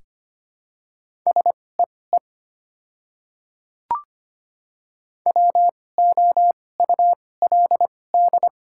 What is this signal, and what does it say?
Morse code, heard as a keyed tone, about 25 wpm, spaced as for 18 wpm, 700 Hz. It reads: SEE WOULD
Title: 2x Character Spacing - Top 100 Words - 25wpm